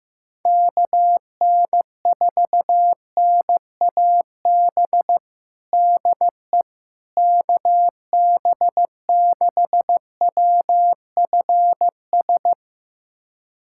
Morse code, heard as a keyed tone, about 15 wpm, 700 Hz. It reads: KN4NAB DE KB6WFS
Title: USA Callsign DE Callsign - 6 Characters - 15wpm